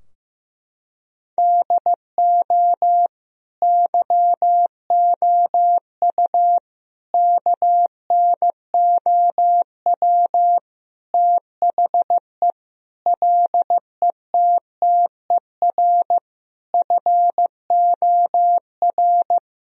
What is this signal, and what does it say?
Morse code, heard as a keyed tone, about 15 wpm, 700 Hz. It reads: DO YOU KNOW THE LETTER FOR
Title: Sentences from Top 1000 Words - Review Part 2 - 15wpm